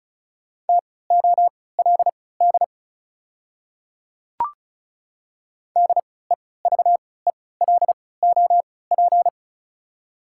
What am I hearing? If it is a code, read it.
Morse code, heard as a keyed tone, about 36 wpm, spaced as for 20 wpm, 690 Hz. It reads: TOLD DEVELOP
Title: 3x Character Spacing - Top 500 Words - 35wpm